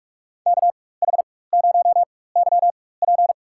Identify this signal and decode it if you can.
Morse code, heard as a keyed tone, about 42 wpm, spaced as for 21 wpm, 700 Hz. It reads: KH0YP
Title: USA Call Signs - 5 Characters - 4x Character Spacing - 45wpm